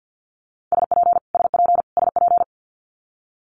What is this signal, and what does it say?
Morse code, heard as a keyed tone, about 50 wpm, 700 Hz. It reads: SP SP SP
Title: Multi-letter Phonemes Repeated 3x 50wpm